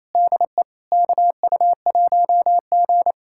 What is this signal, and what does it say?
Morse code, heard as a keyed tone, about 28 wpm, 700 Hz. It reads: DE KU1G
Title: USA Callsign DE Callsign - 4 Characters - 28wpm